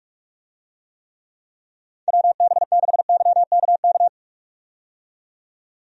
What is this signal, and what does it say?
Morse code, heard as a keyed tone, about 46 wpm, 700 Hz. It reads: WB6YKK
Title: USA Call Signs - 6 Characters - 45wpm